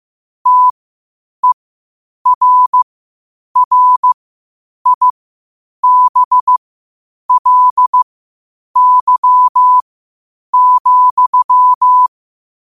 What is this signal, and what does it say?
Morse code, heard as a keyed tone, about 15 wpm, spaced as for 8 wpm, 1000 Hz. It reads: TERRIBLY,
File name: qotdfiles / 8wpm